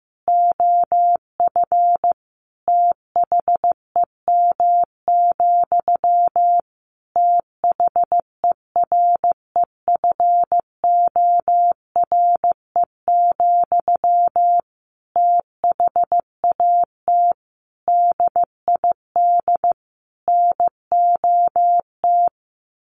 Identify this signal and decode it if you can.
Morse code, heard as a keyed tone, about 15 wpm, 700 Hz. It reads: OF THEM, THEREFORE, THAT DID NOT
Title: Sentences from Top 800 Words 15wpm